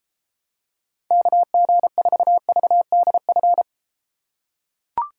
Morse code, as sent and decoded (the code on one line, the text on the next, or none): -.- --. ....- ...- -.. ..-.
KG4VDF